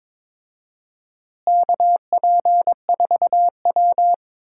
Morse code, heard as a keyed tone, about 21 wpm, 700 Hz. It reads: KP4W
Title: DXCC-100 Call Signs - 4 Characters - Repeated 3x - 22wpm